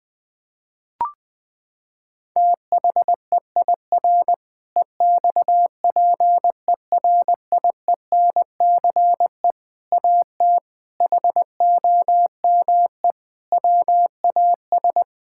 Morse code, heard as a keyed tone, about 21 wpm, 700 Hz. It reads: THEIR EXPERIENCE AT HOME WAS
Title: Sentences from Top 1000 Words - Review Part 6 - 20wpm